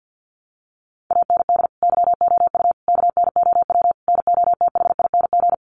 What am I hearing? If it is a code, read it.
Morse code, heard as a keyed tone, about 50 wpm, 700 Hz. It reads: AND YOU KNOW NOTHING
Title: Sentences from Top 1200 Words 50wpm